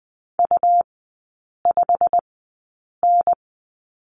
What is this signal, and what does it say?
Morse code, heard as a keyed tone, about 20 wpm, spaced as for 8 wpm, 700 Hz. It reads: U5N
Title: 114 Lesson 19e - Letter w - Three Characters 20wpm